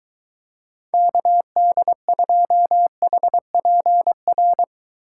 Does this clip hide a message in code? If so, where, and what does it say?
Yes: Morse code from 0.9 to 4.6 s, heard as a keyed tone, about 23 wpm, 700 Hz. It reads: KD2HPR